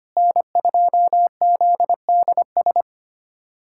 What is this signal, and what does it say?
Morse code, heard as a keyed tone, about 25 wpm, 700 Hz. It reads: N2ZDS
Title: USA Callsign DE Callsign - 5 Characters - 25wpm